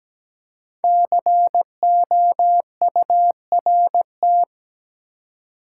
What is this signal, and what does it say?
Morse code, heard as a keyed tone, about 17 wpm, 700 Hz. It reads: COURT
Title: Five Letter Words - 17wpm